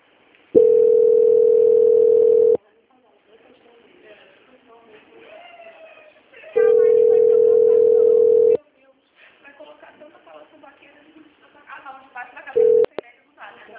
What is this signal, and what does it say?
Ringback tone (North America), 440 + 480 Hz, with speech.
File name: out- -2001- - - .50